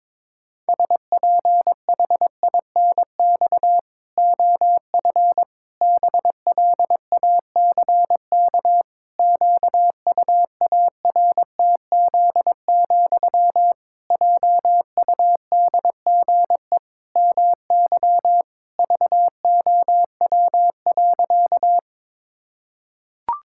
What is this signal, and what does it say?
Morse code, heard as a keyed tone, about 22 wpm, 700 Hz. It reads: SPHINX OF BLACK QUARTZ, JUDGE MY VOW.
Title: Pangram Challenge v2 - 22wpm